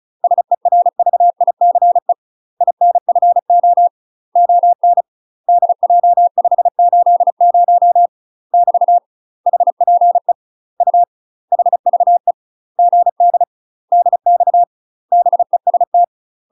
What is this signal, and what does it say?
Morse code, heard as a keyed tone, about 34 wpm, 700 Hz. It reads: SERVICE INFO ON DJ580 <BT> HPE U HVE GD DX BEST
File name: UK MASTER QSOs TRAINING / 35WPM / UK MASTER QSOs 0043